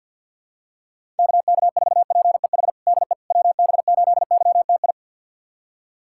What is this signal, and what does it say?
Morse code, heard as a keyed tone, about 48 wpm, 700 Hz. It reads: KK4PEH DE WB8YTI
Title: USA Callsign DE Callsign - 6 Characters - 50wpm